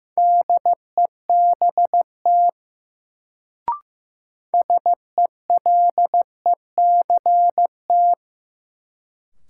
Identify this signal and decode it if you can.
Morse code, heard as a keyed tone, about 15 wpm, 700 Hz. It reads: DEBT SELECT
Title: Sentences from Top 1300 Words - New Word Drill - 15wpm